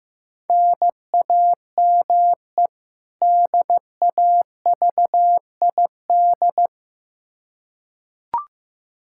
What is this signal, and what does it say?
Morse code, heard as a keyed tone, about 15 wpm, 700 Hz. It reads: NAME DAVID
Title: K1USN SST QSO Practice 15wpm